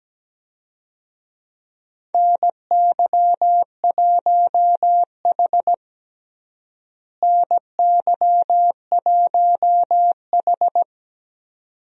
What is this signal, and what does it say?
Morse code, heard as a keyed tone, about 17 wpm, 700 Hz. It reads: NY1H NY1H
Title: USA Call Signs - 4 Characters - Repeated 2x - 17wpm